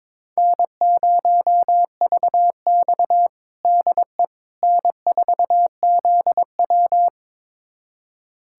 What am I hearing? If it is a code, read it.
Morse code, heard as a keyed tone, about 22 wpm, 700 Hz. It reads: N0VX DE N4ZW
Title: USA Callsign DE Callsign - 4 Characters - 22wpm